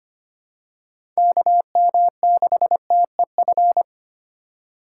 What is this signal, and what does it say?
Morse code, heard as a keyed tone, about 25 wpm, 700 Hz. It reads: KM6TEF
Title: USA Call Signs - 6 Characters - 2x Character Spacing - 25wpm